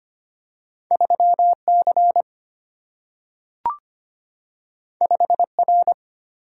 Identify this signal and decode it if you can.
Morse code, heard as a keyed tone, about 25 wpm, 700 Hz. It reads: 3 C 5 R
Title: Single-Letter Single-Number 25wpm